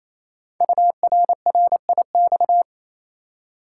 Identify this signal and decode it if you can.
Morse code, heard as a keyed tone, about 28 wpm, 700 Hz. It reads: URRIX
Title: Code Groups - 5 Letters - 28wpm